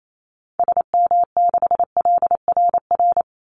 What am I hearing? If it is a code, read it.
Morse code, heard as a keyed tone, about 28 wpm, 700 Hz. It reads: SM6LRR
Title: Call Signs Repeated 2x - 28wpm